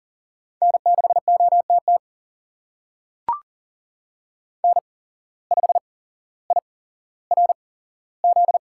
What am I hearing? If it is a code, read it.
Morse code, heard as a keyed tone, about 40 wpm, spaced as for 17 wpm, 700 Hz. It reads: N6OTT N5IRZ
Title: USA Call Signs - 5 Characters - 8x Character Spacing - 40wpm